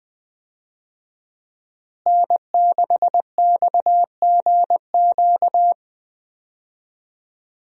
Morse code, heard as a keyed tone, about 20 wpm, 700 Hz. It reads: N6XGQ
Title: USA Call Signs - 5 Characters - Repeated 3x - 20wpm